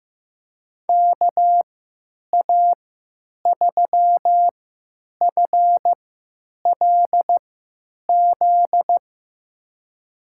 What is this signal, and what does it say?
Morse code, heard as a keyed tone, about 15 wpm, spaced as for 9 wpm, 700 Hz. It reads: KA3FLZ
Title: USA Call Signs - 6 Characters - 3x Character Spacing - 15wpm